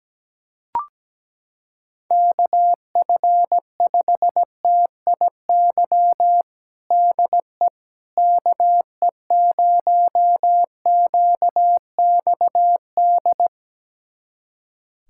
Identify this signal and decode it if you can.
Morse code, heard as a keyed tone, about 17 wpm, 700 Hz. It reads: KF5TIY DE KE0QXD
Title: USA Callsign DE Callsign - 6 Characters - 17wpm